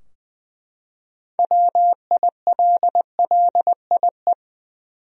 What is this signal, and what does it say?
Morse code, heard as a keyed tone, about 21 wpm, 700 Hz. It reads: WILLIE
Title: USA Common Names - 20wpm